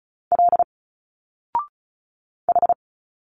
Morse code, heard as a keyed tone, about 34 wpm, 700 Hz. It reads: L H
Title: Single Letters 35wpm